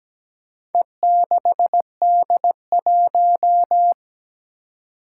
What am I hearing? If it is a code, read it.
Morse code, heard as a keyed tone, about 17 wpm, 700 Hz. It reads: E6D1
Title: Four Challenging Letter-Number Combinations 17wpm